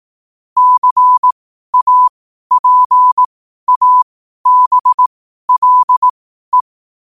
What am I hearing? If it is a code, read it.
Morse code, heard as a keyed tone, about 18 wpm, spaced as for 12 wpm, 1000 Hz. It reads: CAPABLE
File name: qotdfiles / 12wpm